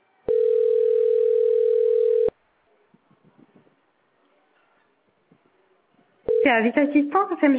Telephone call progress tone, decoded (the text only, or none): ringback tone